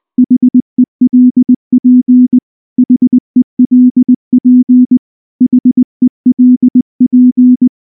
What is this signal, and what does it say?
Morse code, heard as a keyed tone, about 20 wpm, 250 Hz. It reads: HELP HELP HELP
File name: morse / morse-training-20WPM-7